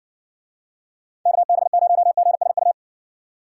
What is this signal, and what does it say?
Morse code, heard as a keyed tone, about 59 wpm, 700 Hz. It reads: KB0GSU